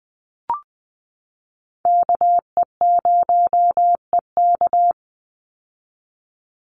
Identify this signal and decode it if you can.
Morse code, heard as a keyed tone, about 20 wpm, 700 Hz. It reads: KE0EK
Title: Call Signs - 20wpm